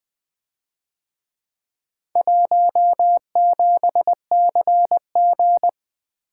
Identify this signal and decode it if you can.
Morse code, heard as a keyed tone, about 21 wpm, 700 Hz. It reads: 17CG
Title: Speed-Racing - Four Letter-Number Combinations - 20wpm